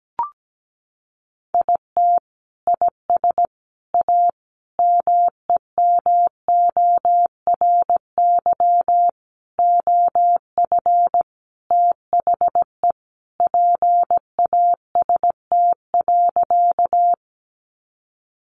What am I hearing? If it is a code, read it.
Morse code, heard as a keyed tone, about 17 wpm, 700 Hz. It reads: IT IS A MEMORY OF THE PAST.
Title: Sentences from Top 1300 Words 17wpm